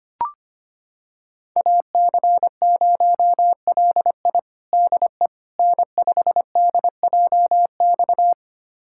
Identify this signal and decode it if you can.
Morse code, heard as a keyed tone, about 25 wpm, 700 Hz. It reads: AC0LI DE N5DJX
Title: USA Callsign DE Callsign 25wpm